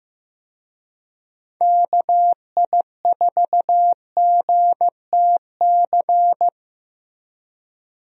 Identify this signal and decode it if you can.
Morse code, heard as a keyed tone, about 15 wpm, 700 Hz. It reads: KI4GTC
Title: USA Call Signs - 6 Characters - Repeated 2x - 15wpm